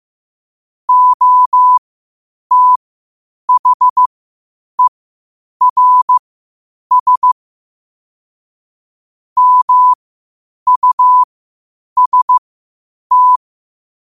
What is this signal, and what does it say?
Morse code, heard as a keyed tone, about 15 wpm, spaced as for 8 wpm, 1000 Hz. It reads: OTHERS MUST
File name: qotdfiles / 8wpm